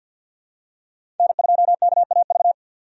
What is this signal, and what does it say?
Morse code, heard as a keyed tone, about 51 wpm, 700 Hz. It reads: N2KAV
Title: USA Call Signs - 5 Characters - 50wpm